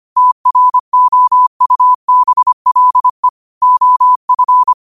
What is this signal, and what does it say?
Morse code, heard as a keyed tone, about 25 wpm, 1000 Hz. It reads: TROUBLE OF
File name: qotdfiles / 25wpm